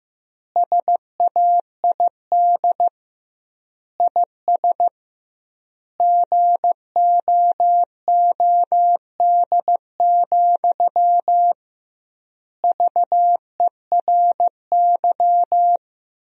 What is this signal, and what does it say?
Morse code, heard as a keyed tone, about 15 wpm, 700 Hz. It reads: SAID IS GOOD, VERY
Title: 2x Word-Spacing - Sentences from Top 200 Words - 15wpm